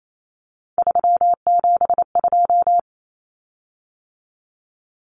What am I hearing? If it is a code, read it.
Morse code, heard as a keyed tone, about 28 wpm, 700 Hz. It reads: 372
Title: Common Cut Numbers - 3-Digit 28wpm